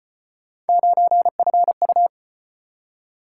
Morse code, heard as a keyed tone, about 35 wpm, 700 Hz. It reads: 9FU